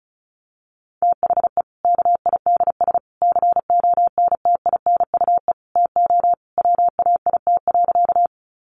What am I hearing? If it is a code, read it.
Morse code, heard as a keyed tone, about 34 wpm, 700 Hz. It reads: THE KIDS CONTINUE TO WAIT.
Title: Easy Phrases 35wpm